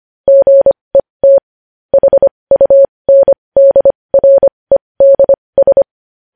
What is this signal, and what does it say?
Morse code, heard as a keyed tone, about 25 wpm, spaced as for 20 wpm, 550 Hz. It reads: GET HUNDREDS
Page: News Headlines in Morse Code at 20 WPM